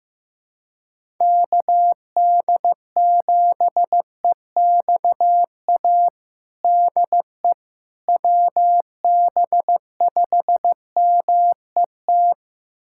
Morse code, heard as a keyed tone, about 16 wpm, 700 Hz. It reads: KD7EXA DE WB5MET
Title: USA Callsign DE Callsign - 6 Characters - 15wpm